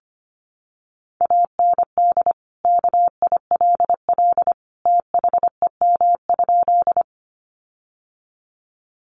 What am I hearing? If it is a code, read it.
Morse code, heard as a keyed tone, about 25 wpm, 700 Hz. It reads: AND KILL THEM?
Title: Sentences from Top 1400 Words 25wpm